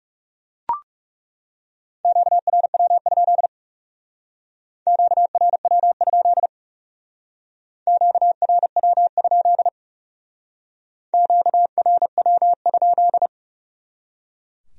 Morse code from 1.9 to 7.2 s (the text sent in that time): QRW? QRW?